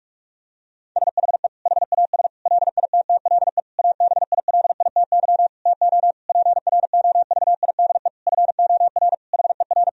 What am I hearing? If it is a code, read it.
Morse code, heard as a keyed tone, about 43 wpm, 700 Hz. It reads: SHE HAS LITTLE ABILITY TO PROVIDE FOR HER
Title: Sentences from Top 1000 Words - Review Part 3 - 45wpm